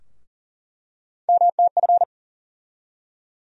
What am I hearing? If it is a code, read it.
Morse code, heard as a keyed tone, about 40 wpm, 700 Hz. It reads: MTF
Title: USA Call Sign Postfixes - 3 Letters 40wpm